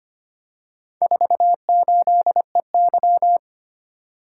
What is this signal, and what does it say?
Morse code, heard as a keyed tone, about 25 wpm, 700 Hz. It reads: 48EY
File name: practice / 25wpm / Four Letter-Number Combinations - 25wpm